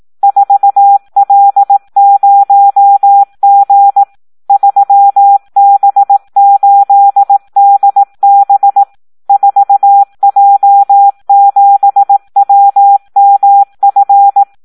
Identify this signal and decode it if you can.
Morse code, heard as a keyed tone, about 18 wpm, 800 Hz. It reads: 4L0G 3B8DB 4J7WMF
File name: Audio practica CW / SD 1 18